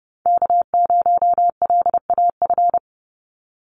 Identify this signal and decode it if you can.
Morse code, heard as a keyed tone, about 30 wpm, 700 Hz. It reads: K0LAF
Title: NA SOTA Chaser Callsigns 30wpm